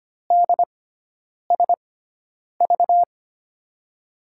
Morse code, heard as a keyed tone, about 25 wpm, spaced as for 9 wpm, 700 Hz. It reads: DSV
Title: USA Call Signs - 6 Characters - 6x Character Spacing - 25wpm